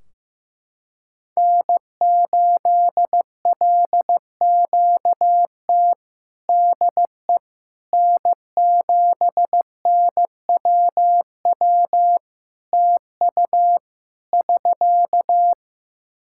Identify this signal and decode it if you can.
Morse code, heard as a keyed tone, about 15 wpm, 700 Hz. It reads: N8LQT DE N7NWW TU <SK>
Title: Practice QSO - Exchange 4 - Ending - 15wpm